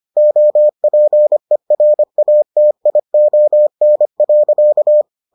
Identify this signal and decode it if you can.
Morse code, heard as a keyed tone, about 25 wpm, 600 Hz. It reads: OPERATION.